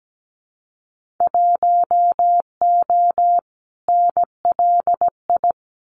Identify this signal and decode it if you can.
Morse code, heard as a keyed tone, about 17 wpm, 700 Hz. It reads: 1O NLI
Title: Winter Field Day Exchanges 17wpm